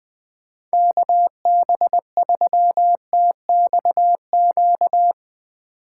Morse code, heard as a keyed tone, about 23 wpm, 700 Hz. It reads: KB3TXQ
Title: Speed Racing - USA 2x3 Call Signs 20wpm